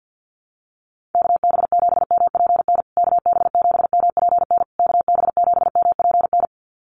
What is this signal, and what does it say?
Morse code, heard as a keyed tone, about 50 wpm, 700 Hz. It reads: KB7MPN KB7MPN KB7MPN
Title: USA Call Signs Repeated 3x 50wpm